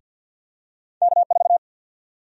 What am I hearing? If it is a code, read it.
Morse code, heard as a keyed tone, about 51 wpm, 700 Hz. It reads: K4
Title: USA Call Sign Prefixes - 50wpm